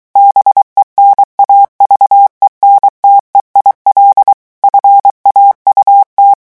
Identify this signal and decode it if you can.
Morse code, heard as a keyed tone, about 23 wpm, 800 Hz. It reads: BENAVENTE IL FAUT
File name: morse mp3 / revision9 citations humoristiques vitesse 22 mots minute